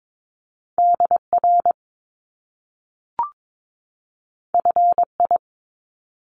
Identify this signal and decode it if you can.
Morse code, heard as a keyed tone, about 22 wpm, 700 Hz. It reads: DR FI
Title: Top Two Letter Combinations in Words 22wpm